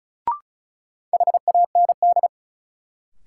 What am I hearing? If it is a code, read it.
Morse code, heard as a keyed tone, about 36 wpm, 700 Hz. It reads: HAND
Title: Sentences from Top 200 Words - New Word Drill - 35wpm